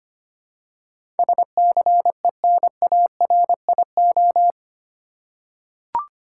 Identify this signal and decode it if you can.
Morse code, heard as a keyed tone, about 25 wpm, 700 Hz. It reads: SCENARIO
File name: Eight Letter Words - 25wpm